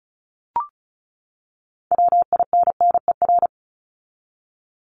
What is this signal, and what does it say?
Morse code, heard as a keyed tone, about 34 wpm, 700 Hz. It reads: WINNER
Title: Top 1751-2000 Words 35wpm